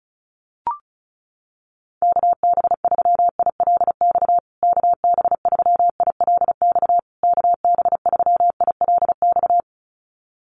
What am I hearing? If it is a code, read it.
Morse code, heard as a keyed tone, about 34 wpm, 700 Hz. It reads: KB3ILX KB3ILX KB3ILX
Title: USA Call Signs Repeated 3x 35wpm